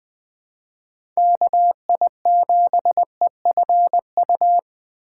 Morse code, heard as a keyed tone, about 20 wpm, 700 Hz. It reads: KI7EFU
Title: USA Call Signs - 6 Characters - 4x Character Spacing - 20wpm